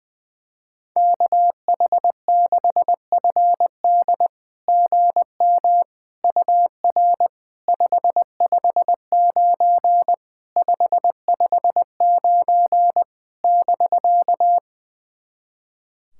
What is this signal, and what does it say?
Morse code, heard as a keyed tone, about 20 wpm, 690 Hz. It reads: KH6FD GM UR 559 559 <BK>
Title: SOTA QSO Practice 20wpm